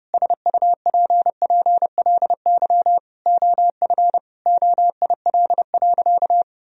Morse code, heard as a keyed tone, about 32 wpm, 700 Hz. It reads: SUPPLY OF OIL.
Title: Sentences from Top 1000 Words - Review Part 4 - 30wpm